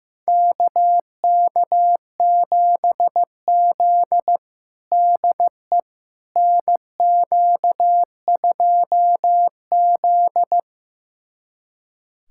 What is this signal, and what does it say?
Morse code, heard as a keyed tone, about 15 wpm, 700 Hz. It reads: KK7Z DE NQ2Z
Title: USA Callsign DE Callsign - 4 Characters - 15wpm